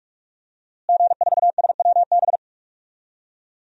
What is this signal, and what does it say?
Morse code, heard as a keyed tone, about 46 wpm, 700 Hz. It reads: G4SWB
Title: DXCC-100 Call Signs - 5 Characters - Repeated 2x - 45wpm